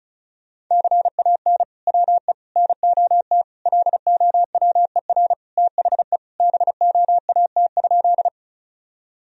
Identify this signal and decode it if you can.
Morse code, heard as a keyed tone, about 34 wpm, 700 Hz. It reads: CAN WE NOT LOWER THE BOAT?
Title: Sentences from Top 1000 Words - Review Part 4 - 35wpm